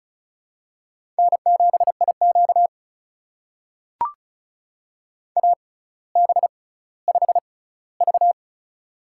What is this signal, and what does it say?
Morse code, heard as a keyed tone, about 36 wpm, spaced as for 18 wpm, 700 Hz. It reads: N7IQ AB5V
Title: USA Call Signs - 4 Characters - 6x Character Spacing - 35wpm